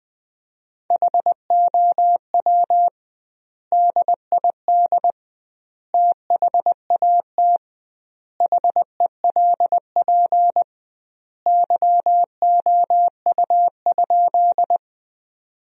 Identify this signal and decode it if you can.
Morse code, heard as a keyed tone, about 20 wpm, 700 Hz. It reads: HOW DID THAT HELP YOU?
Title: 2x Word-Spacing - Sentences from Top 200 Words - 20wpm